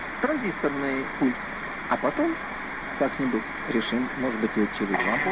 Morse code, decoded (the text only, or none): none